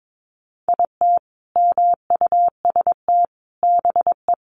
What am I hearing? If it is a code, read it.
Morse code, heard as a keyed tone, about 22 wpm, 700 Hz. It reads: IT MUST BE
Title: Sentences from Top 1400 Words 22wpm